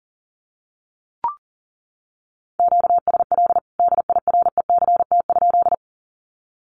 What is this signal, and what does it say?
Morse code, heard as a keyed tone, about 40 wpm, 700 Hz. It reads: QSL DIRECT?
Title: Common QSO Phrases 40wpm